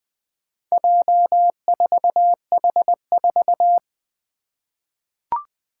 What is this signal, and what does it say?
Morse code, heard as a keyed tone, about 20 wpm, 700 Hz. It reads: J4H4
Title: Four Challenging Letter-Number Combinations 20wpm